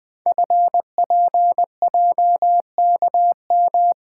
Morse code, heard as a keyed tone, about 20 wpm, 700 Hz. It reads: FPJKM
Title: Code Groups - 5 Letter:Numbers - 20wpm